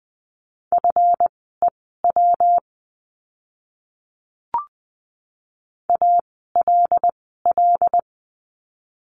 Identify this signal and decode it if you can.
Morse code, heard as a keyed tone, about 20 wpm, spaced as for 15 wpm, 700 Hz. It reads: FEW ALL
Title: 237 Lesson 38f - Letter q - Words 20wpm